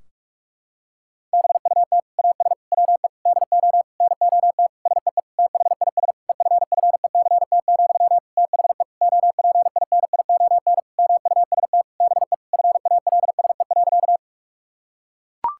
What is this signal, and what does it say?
Morse code, heard as a keyed tone, about 42 wpm, 700 Hz. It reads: BUT AS WE DO NOT SEE THIS EFFECT, THE OPINION MUST BE FALSE.
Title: Sentences from Top 1600 Words 45wpm